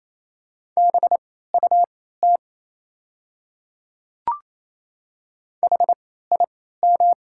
Morse code, heard as a keyed tone, about 27 wpm, spaced as for 16 wpm, 690 Hz. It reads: BUT HIM
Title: 3x Character Spacing - Top 100 Words - 28wpm